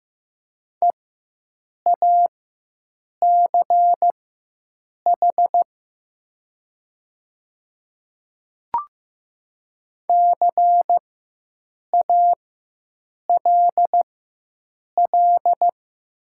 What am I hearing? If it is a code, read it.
Morse code, heard as a keyed tone, about 15 wpm, spaced as for 7 wpm, 700 Hz. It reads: EACH CALL
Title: 4x Character Spacing - Top 100 Words - 15wpm